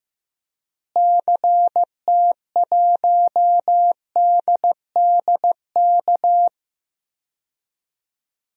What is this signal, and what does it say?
Morse code, heard as a keyed tone, about 15 wpm, 700 Hz. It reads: CT1DDK
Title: DXCC-100 Call Signs - 6 Characters - Repeated 3x - 15wpm